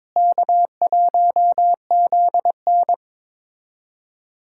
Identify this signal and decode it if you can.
Morse code, heard as a keyed tone, about 21 wpm, 700 Hz. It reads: K1ZN
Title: K1USN SST QSO Practice - 22wpm